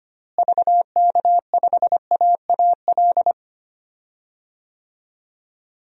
Morse code, heard as a keyed tone, about 25 wpm, 700 Hz. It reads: VK5AAL
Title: DXCC-100 Call Signs - 6 Characters - Repeated 2x - 25wpm